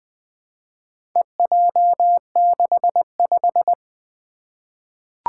Morse code, heard as a keyed tone, about 20 wpm, 700 Hz. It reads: EJ65
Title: Four Challenging Letter-Number Combinations 20wpm